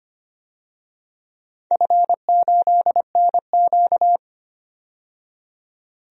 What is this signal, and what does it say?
Morse code, heard as a keyed tone, about 25 wpm, 700 Hz. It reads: F8NQ
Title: DXCC-100 Call Signs - 4 Characters - 25wpm